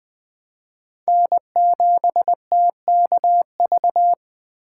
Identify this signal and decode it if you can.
Morse code, heard as a keyed tone, about 20 wpm, 700 Hz. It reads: N7TKV
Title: USA Call Signs - 5 Characters - 3x Character Spacing - 20wpm